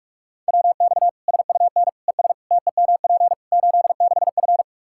Morse code, heard as a keyed tone, about 46 wpm, 700 Hz. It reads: WX SUN ES TEMP 86F
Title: K1USN SST QSO Practice 45wpm